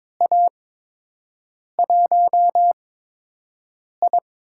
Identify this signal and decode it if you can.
Morse code, heard as a keyed tone, about 22 wpm, spaced as for 6 wpm, 700 Hz. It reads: A1I
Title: USA Call Signs - 8x Character Spacing - 22wpm